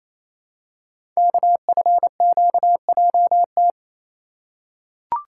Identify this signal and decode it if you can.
Morse code, heard as a keyed tone, about 29 wpm, 690 Hz. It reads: KFQJT
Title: Code Groups - 5 Letters - 28wpm